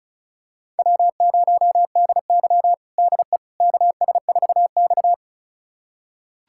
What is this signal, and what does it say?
Morse code, heard as a keyed tone, about 36 wpm, 700 Hz. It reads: W0DY DE KS4X
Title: USA Callsign DE Callsign - 4 Characters - 35wpm